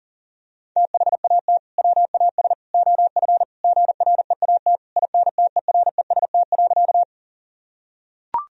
Morse code, heard as a keyed tone, about 40 wpm, 700 Hz. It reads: THAT WAS OF GREAT INTEREST.
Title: Sentences from Top 1000 Words - Review Part 2 - 40wpm